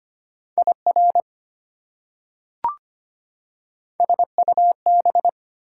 Morse code, heard as a keyed tone, about 25 wpm, 700 Hz. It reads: IR SUB
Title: Prefixes and Suffixes 25wpm